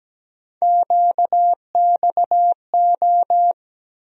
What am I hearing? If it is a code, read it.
Morse code, heard as a keyed tone, about 17 wpm, 700 Hz. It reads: QXO
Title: USA Call Sign Postfixes - 3 Letters 17wpm